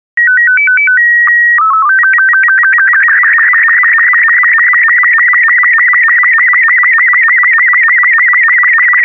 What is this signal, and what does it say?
SSTV (slow-scan television), Robot 36.